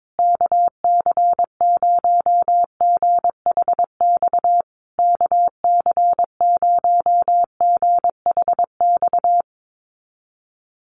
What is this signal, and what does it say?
Morse code, heard as a keyed tone, about 22 wpm, 700 Hz. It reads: KC0GHX KC0GHX
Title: USA Call Signs Repeated 3x 22wpm